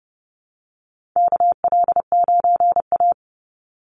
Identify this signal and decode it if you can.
Morse code, heard as a keyed tone, about 30 wpm, 700 Hz. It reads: KL9A
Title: Call Signs Repeated 2x - 30wpm